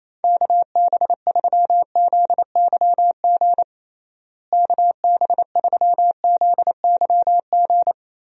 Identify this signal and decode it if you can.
Morse code, heard as a keyed tone, about 28 wpm, 700 Hz. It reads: KB3ZYG KB3ZYG
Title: USA Call Signs - 6 Characters - Repeated 3x - 28wpm